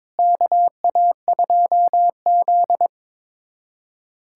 Morse code, heard as a keyed tone, about 22 wpm, 700 Hz. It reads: KA2Z
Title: USA Callsign DE Callsign - 4 Characters - 22wpm